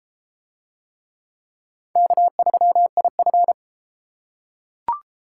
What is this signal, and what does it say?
Morse code, heard as a keyed tone, about 29 wpm, 700 Hz. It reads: K3IF